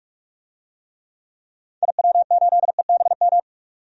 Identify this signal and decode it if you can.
Morse code, heard as a keyed tone, about 46 wpm, 700 Hz. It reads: IW8EBM
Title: DXCC-100 Call Signs - 6 Characters - 45wpm